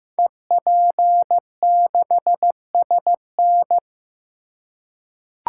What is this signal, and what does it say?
Morse code, heard as a keyed tone, about 15 wpm, 700 Hz. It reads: EP6SN
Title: Code Groups - 5 Letter:Numbers - 15wpm